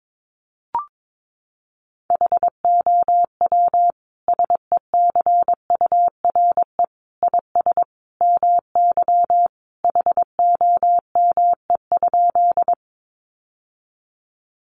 Morse code, heard as a keyed tone, about 22 wpm, 700 Hz. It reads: HOW SECURE IS MY HOME?
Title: Sentences from Top 1200 Words 22wpm